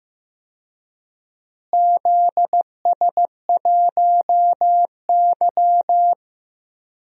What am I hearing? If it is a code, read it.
Morse code, heard as a keyed tone, about 15 wpm, 700 Hz. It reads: ZS1Y
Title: DXCC-100 Call Signs - 4 Characters - Repeated 2x - 15wpm